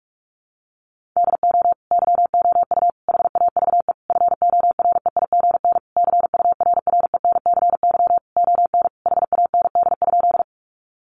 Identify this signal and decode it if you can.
Morse code, heard as a keyed tone, about 44 wpm, 700 Hz. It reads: DO YOU HAVE FOREIGN CURRENCY ON HAND?
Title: Sentences from Top 1500 Words 45wpm